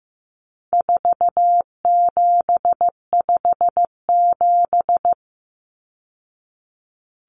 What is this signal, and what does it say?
Morse code, heard as a keyed tone, about 15 wpm, 700 Hz. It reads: 4757
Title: Numbers - 4-Digit 15wpm